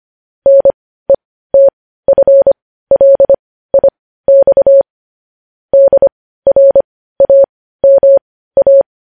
Morse code, heard as a keyed tone, about 25 wpm, spaced as for 15 wpm, 550 Hz. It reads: NETFLIX DRAMA
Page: news-headlines-in-morse-code-at-15-wpm